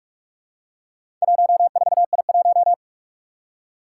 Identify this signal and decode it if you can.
Morse code, heard as a keyed tone, about 44 wpm, 700 Hz. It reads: 14I1
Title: Four Challenging Letter-Number Combinations 45wpm